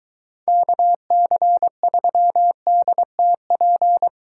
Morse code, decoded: KC3DTP

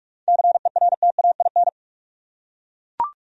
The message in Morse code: CERTAIN

